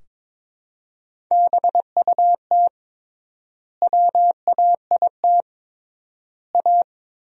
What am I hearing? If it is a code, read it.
Morse code, heard as a keyed tone, about 22 wpm, 700 Hz. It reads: BUT WAIT A
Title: 3x Word-Spacing - Sentences from Top 500 Words - 22wpm